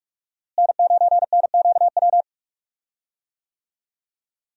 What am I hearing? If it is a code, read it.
Morse code, heard as a keyed tone, about 42 wpm, 700 Hz. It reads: N9NQW
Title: USA Call Signs - 5 Characters - Repeated 3x - 45wpm